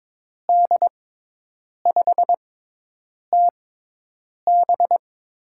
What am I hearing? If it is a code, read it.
Morse code, heard as a keyed tone, about 22 wpm, spaced as for 8 wpm, 700 Hz. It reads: D5TB
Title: USA Call Signs - 5 Characters - 6x Character Spacing - 22wpm